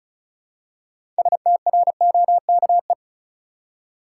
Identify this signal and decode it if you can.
Morse code, heard as a keyed tone, about 36 wpm, 700 Hz. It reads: STROKE
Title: Six Letter Words - 35wpm